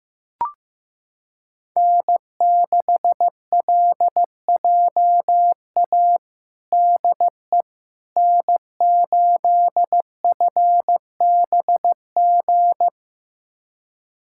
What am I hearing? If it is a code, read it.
Morse code, heard as a keyed tone, about 15 wpm, 700 Hz. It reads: N6LJA DE N8FBG
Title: USA Callsign DE Callsign - 5 Characters - 15wpm